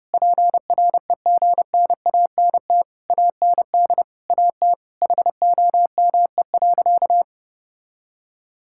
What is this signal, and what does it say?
Morse code, heard as a keyed tone, about 30 wpm, 700 Hz. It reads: PREGNANT AND AT HOME.
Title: Sentences from Top 1000 Words - Review Part 6 - 30wpm